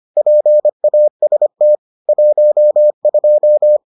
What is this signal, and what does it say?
Morse code, heard as a keyed tone, about 25 wpm, 600 Hz. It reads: PAST 12